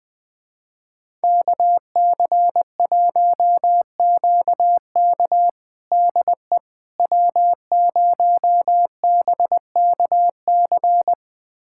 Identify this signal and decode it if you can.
Morse code, heard as a keyed tone, about 20 wpm, 700 Hz. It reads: KC1QK DE W0BKC
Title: USA Callsign DE Callsign - 5 Characters - 20wpm